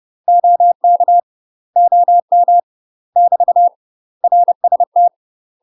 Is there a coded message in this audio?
Yes: Morse code from 0.3 to 5.1 s, heard as a keyed tone, about 30 wpm, 700 Hz. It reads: OK OM <BT> RST